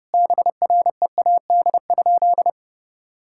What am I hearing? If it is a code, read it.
Morse code, heard as a keyed tone, about 30 wpm, 700 Hz. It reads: BREAD?
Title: Sentences from Top 1600 Words 30wpm